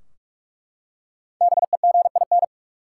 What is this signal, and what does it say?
Morse code, heard as a keyed tone, about 42 wpm, 700 Hz. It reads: BEGIN
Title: Five Letter Words - 45wpm